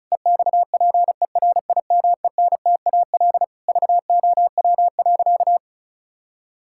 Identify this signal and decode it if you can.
Morse code, heard as a keyed tone, about 36 wpm, 700 Hz. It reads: EXPERIMENTAL VOW.
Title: Pangram Challenge v2 - 35wpm